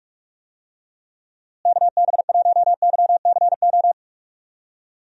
Morse code, heard as a keyed tone, about 46 wpm, 700 Hz. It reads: KB1YCO
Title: USA Call Signs - 6 Characters - Repeated 2x - 45wpm